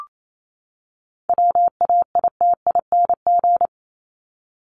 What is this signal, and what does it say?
Morse code, heard as a keyed tone, about 28 wpm, 700 Hz. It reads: WAITING
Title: ING Suffix - Top 100 Words with Base Words 28wpm